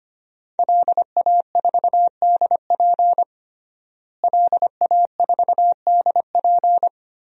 Morse code, heard as a keyed tone, about 25 wpm, 700 Hz. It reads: LA4DP LA4DP
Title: DXCC-100 Call Signs - 5 Characters - Repeated 2x - 25wpm